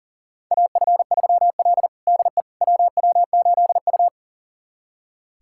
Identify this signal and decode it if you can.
Morse code, heard as a keyed tone, about 40 wpm, 700 Hz. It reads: AF3L DE WW8U
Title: USA Callsign DE Callsign - 4 Characters - 40wpm